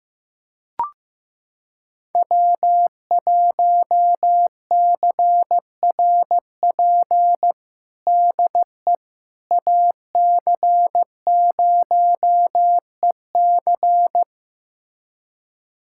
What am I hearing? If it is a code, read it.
Morse code, heard as a keyed tone, about 15 wpm, 700 Hz. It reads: W1CRP DE AC0EC
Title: USA Callsign DE Callsign - 5 Characters - 15wpm